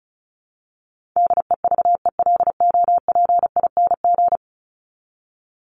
Morse code, heard as a keyed tone, about 34 wpm, 700 Hz. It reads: DEVELOPING
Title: ING Suffix - Top 100 Words with Base Words 35wpm